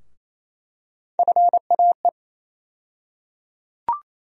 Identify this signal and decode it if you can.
Morse code, heard as a keyed tone, about 28 wpm, 700 Hz. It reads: FAE
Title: USA Call Sign Postfixes - 3 Letters 28wpm